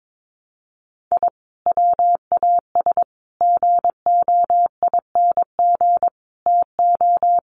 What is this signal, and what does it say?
Morse code, heard as a keyed tone, about 22 wpm, 700 Hz. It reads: I WAS GOING TO
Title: Sets of 5 Words 22wpm